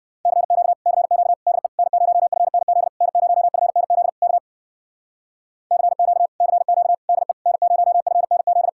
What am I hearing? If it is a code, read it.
Morse code, heard as a keyed tone, about 62 wpm, 700 Hz. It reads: CQ CQ DE N0FNZ N0FNZ K CQ CQ DE N0FNZ
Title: Speed-Racing Common QSO Phrases - 50wpm